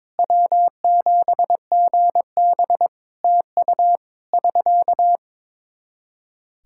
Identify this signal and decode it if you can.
Morse code, heard as a keyed tone, about 22 wpm, 700 Hz. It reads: W7GB TU <SK>
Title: Practice QSO - Exchange 4 - Ending - 22wpm